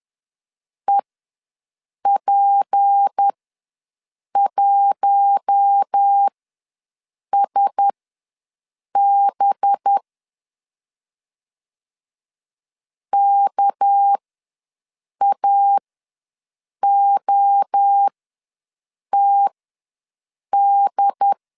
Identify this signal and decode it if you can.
Morse code, heard as a keyed tone, about 10 wpm, spaced as for 6 wpm, 800 Hz. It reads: EP1SB KAOTD
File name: LEZIONI / 6 PKX0 / 6 PKX ws 10wpm 3706